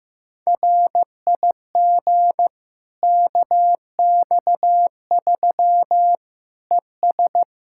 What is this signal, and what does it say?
Morse code, heard as a keyed tone, about 15 wpm, 700 Hz. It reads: RIG KX3 ES
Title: Practice QSO - Exchange 2 - Info - 15wpm